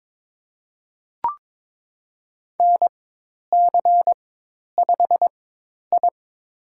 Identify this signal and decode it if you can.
Morse code, heard as a keyed tone, about 23 wpm, spaced as for 10 wpm, 700 Hz. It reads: NC5I
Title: USA Call Signs - 4 Characters - 4x Character Spacing - 22wpm